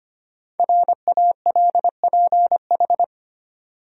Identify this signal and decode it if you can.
Morse code, heard as a keyed tone, about 25 wpm, 700 Hz. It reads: RALPH
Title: USA Common Names - Max 5 Letters - 25wpm